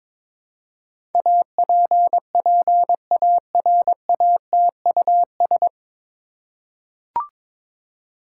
Morse code, heard as a keyed tone, about 23 wpm, 700 Hz. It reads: APPARATUS
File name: 9 Letter Words - First 3 Letters Given - 22wpm